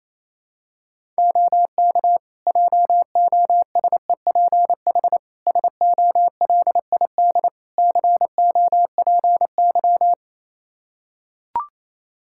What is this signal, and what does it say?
Morse code, heard as a keyed tone, about 28 wpm, 700 Hz. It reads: OK JOSEPH SOLID COPY
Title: Practice QSO - Exchange 3 - More Info - 28wpm